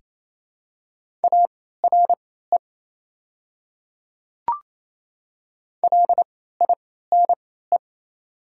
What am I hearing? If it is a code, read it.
Morse code, heard as a keyed tone, about 27 wpm, spaced as for 16 wpm, 690 Hz. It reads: ARE LINE
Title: 3x Character Spacing - Top 200 Words - 28wpm